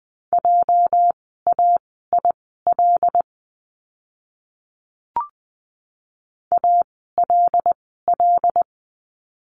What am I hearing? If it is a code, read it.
Morse code, heard as a keyed tone, about 20 wpm, spaced as for 15 wpm, 700 Hz. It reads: JAIL ALL
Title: 206 Lesson 33f - Letter j - Words 20wpm